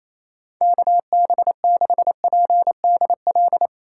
Morse code, heard as a keyed tone, about 34 wpm, 700 Hz. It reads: KB6PDL
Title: Speed Racing - USA Call Signs 28wpm